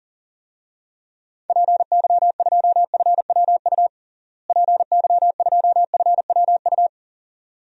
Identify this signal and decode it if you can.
Morse code, heard as a keyed tone, about 40 wpm, 700 Hz. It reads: PY2FWU PY2FWU
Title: DXCC-100 Call Signs - 6 Characters - Repeated 2x - 40wpm